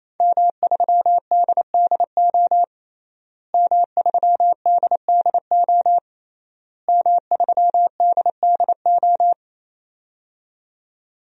Morse code, heard as a keyed tone, about 28 wpm, 700 Hz. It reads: M3DDO M3DDO M3DDO
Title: DXCC-100 Call Signs - 5 Characters - Repeated 3x - 28wpm